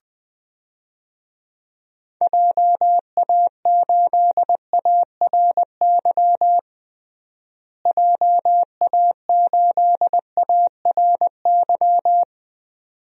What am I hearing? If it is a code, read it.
Morse code, heard as a keyed tone, about 20 wpm, 700 Hz. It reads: JA8ARY JA8ARY
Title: DXCC-100 Call Signs - 6 Characters - Repeated 2x - 20wpm